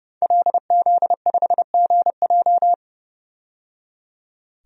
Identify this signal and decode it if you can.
Morse code, heard as a keyed tone, about 30 wpm, 700 Hz. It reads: LZ5GJ
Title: DXCC-100 Call Signs - 5 Characters - 30wpm